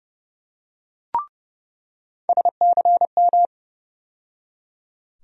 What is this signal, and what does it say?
Morse code, heard as a keyed tone, about 30 wpm, 700 Hz. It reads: SCM
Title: USA Call Sign Postfixes - 3 Letters 30wpm